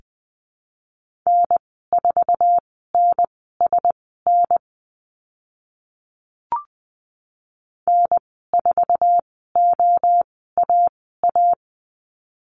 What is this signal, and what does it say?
Morse code, heard as a keyed tone, about 20 wpm, spaced as for 15 wpm, 700 Hz. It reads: N4NSN N4OAA
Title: 048 Lesson 9g - Number 4 - Call Signs 20wpm